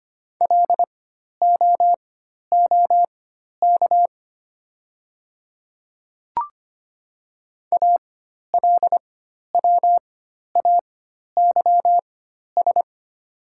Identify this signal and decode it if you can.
Morse code, heard as a keyed tone, about 25 wpm, spaced as for 12 wpm, 700 Hz. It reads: LOOK ALWAYS
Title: 4x Character Spacing - Top 300 Words - 25wpm